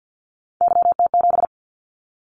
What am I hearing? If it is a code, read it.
Morse code, heard as a keyed tone, about 50 wpm, 700 Hz. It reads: YT7
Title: Callsign Prefixes for Top 25 DXCC - 50wpm